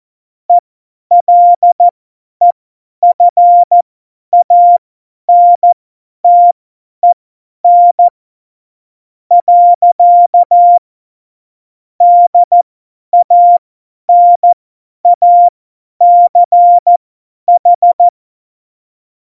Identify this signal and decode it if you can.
Morse code, heard as a keyed tone, about 14 wpm, spaced as for 10 wpm, 700 Hz. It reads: ELEFANTEN . DANACH
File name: Morsekurs / Witze / Witz 18. Dezember, 50 F70